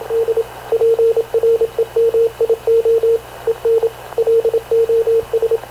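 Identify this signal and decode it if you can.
Morse code, heard as a keyed tone, about 26 wpm, 460 Hz, with radio noise. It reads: D PREMIO EN LOS
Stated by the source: Havana Radio CLA press (PX) broadcast, recorded by Radio Officer Paul Dumesnil aboard mv CNAV QUEST/CZDO.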